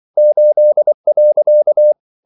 Morse code, heard as a keyed tone, about 24 wpm, 600 Hz. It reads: 8.
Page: Ferie 24 wpm